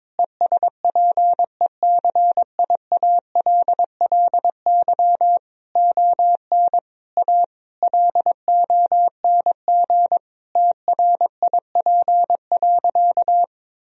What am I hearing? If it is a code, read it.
Morse code, heard as a keyed tone, about 22 wpm, 700 Hz. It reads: ESPECIALLY ON A LONG TRIP.
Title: Sentences from Top 1000 Words - Review Part 4 - 22wpm